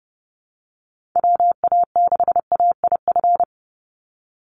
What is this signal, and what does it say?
Morse code, heard as a keyed tone, about 30 wpm, 700 Hz. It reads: WA6AIF
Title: USA Call Signs 30wpm